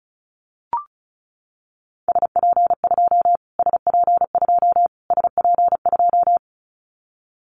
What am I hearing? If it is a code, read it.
Morse code, heard as a keyed tone, about 34 wpm, 700 Hz. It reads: SP2 SP2 SP2
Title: Callsign Prefixes for Top 25 DXCC - Repeated 3x - 35wpm